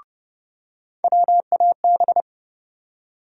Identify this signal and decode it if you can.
Morse code, heard as a keyed tone, about 30 wpm, 700 Hz. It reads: WAB
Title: USA Call Sign Postfixes - 3 Letters 30wpm